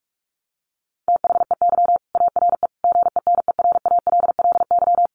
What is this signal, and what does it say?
Morse code, heard as a keyed tone, about 44 wpm, 700 Hz. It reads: THEY ARE GENERALLY
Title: Sentences from Top 600 Words 45wpm